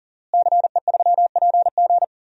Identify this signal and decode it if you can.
Morse code, heard as a keyed tone, about 40 wpm, 700 Hz. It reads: CE3PG
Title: DXCC-100 Call Signs - 5 Characters - Repeated 2x - 40wpm